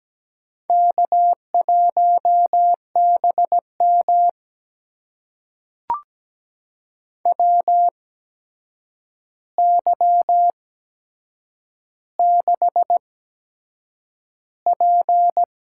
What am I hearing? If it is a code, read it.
Morse code, heard as a keyed tone, about 17 wpm, spaced as for 7 wpm, 700 Hz. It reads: K1BM WY6P
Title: USA Call Signs - 8x Character Spacing - 17wpm